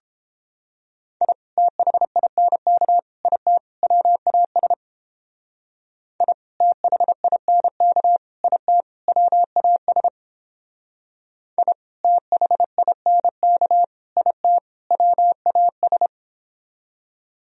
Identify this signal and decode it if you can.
Morse code, heard as a keyed tone, about 29 wpm, 700 Hz. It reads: I THINK IT WAS I THINK IT WAS I THINK IT WAS